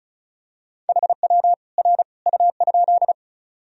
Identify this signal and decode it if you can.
Morse code, heard as a keyed tone, about 36 wpm, 700 Hz. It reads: HW R U?
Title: Sentences from Top 100 Words and Abbreviations 35wpm